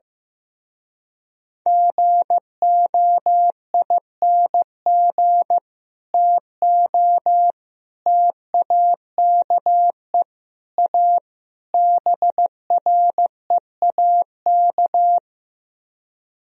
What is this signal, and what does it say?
Morse code, heard as a keyed tone, about 16 wpm, 700 Hz. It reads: GOING TO TAKE A BREAK
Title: Speed-Racing - Sets of 5 Words 15wpm